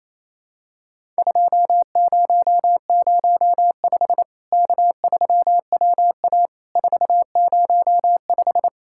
Morse code, heard as a keyed tone, about 28 wpm, 700 Hz. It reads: 2005 K3WA 405
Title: Worked All Europe DX Contest - QTC Exchange - 28wpm